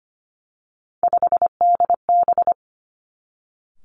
Morse code, heard as a keyed tone, about 25 wpm, 700 Hz. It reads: 5DB
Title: Three Challenging Letter-Number Combinations 25wpm